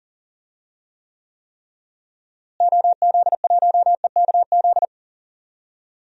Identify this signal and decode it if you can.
Morse code, heard as a keyed tone, about 40 wpm, 700 Hz. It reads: OZ1EKZ
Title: DXCC-100 Call Signs - 6 Characters - Repeated 3x - 40wpm